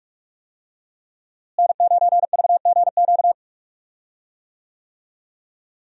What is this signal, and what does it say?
Morse code, heard as a keyed tone, about 46 wpm, 700 Hz. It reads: N9VGQ
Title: USA Call Signs - 5 Characters - Repeated 2x - 45wpm